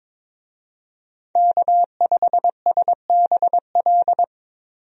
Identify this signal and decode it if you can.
Morse code, heard as a keyed tone, about 21 wpm, 700 Hz. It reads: K5SBL
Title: USA Call Signs - 5 Characters - Repeated 2x - 22wpm